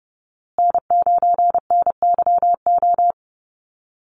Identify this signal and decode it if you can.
Morse code, heard as a keyed tone, about 30 wpm, 700 Hz. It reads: N9NYO
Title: USA Call Signs Repeated 2x 30wpm